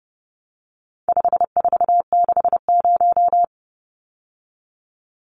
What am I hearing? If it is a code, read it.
Morse code, heard as a keyed tone, about 30 wpm, 700 Hz. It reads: 5460
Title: Numbers - 4-Digit 30wpm